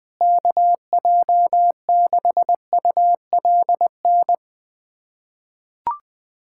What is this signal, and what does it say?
Morse code, heard as a keyed tone, about 20 wpm, 700 Hz. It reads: KJ6ULN
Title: USA Callsign DE Callsign - 6 Characters - 20wpm